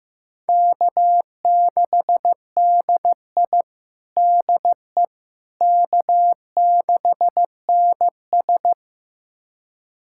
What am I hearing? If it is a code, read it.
Morse code, heard as a keyed tone, about 15 wpm, 700 Hz. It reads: K6DI DE K6NS
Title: USA Callsign DE Callsign - 4 Characters - 15wpm